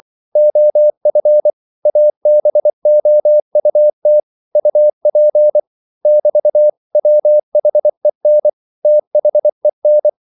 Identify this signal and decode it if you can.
Morse code, heard as a keyed tone, about 24 wpm, 600 Hz. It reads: OF ABOUT UP <BT> WHEN THEN
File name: CW / Practice / 500MostCommonEnglishWords24